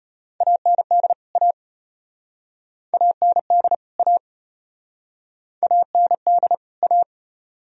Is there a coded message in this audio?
Yes: Morse code from 0.4 to 7.0 s, heard as a keyed tone, about 36 wpm, 700 Hz. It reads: AND A AND A AND A